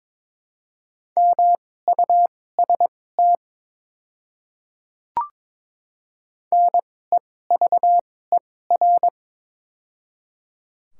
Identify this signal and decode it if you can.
Morse code, heard as a keyed tone, about 22 wpm, spaced as for 16 wpm, 700 Hz. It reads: MUST NEVER
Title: 2x Character Spacing - Top 300 Words - 22wpm